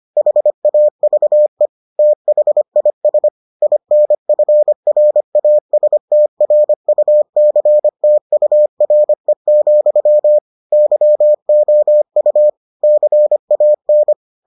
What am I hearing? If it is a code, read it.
Morse code, heard as a keyed tone, about 25 wpm, 600 Hz. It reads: HAVE THIS INFRASTRUCTURE, YOU CAN